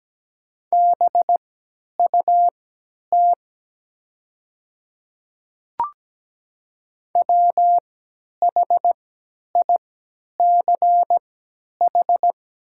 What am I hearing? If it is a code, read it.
Morse code, heard as a keyed tone, about 17 wpm, spaced as for 10 wpm, 700 Hz. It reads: BUT WHICH
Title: 3x Character Spacing - Top 100 Words - 17wpm